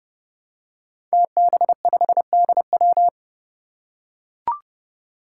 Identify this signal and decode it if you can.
Morse code, heard as a keyed tone, about 30 wpm, 700 Hz. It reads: TB5DW
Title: Five Challenging Letter-Number Combinations 30wpm